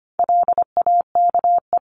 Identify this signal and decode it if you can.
Morse code, heard as a keyed tone, about 25 wpm, 700 Hz. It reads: LAKE
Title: Top 1001-1250 Words 25wpm